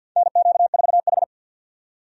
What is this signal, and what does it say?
Morse code, heard as a keyed tone, about 53 wpm, 700 Hz. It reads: NQ4H
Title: USA Call Signs - 4 Characters - 50wpm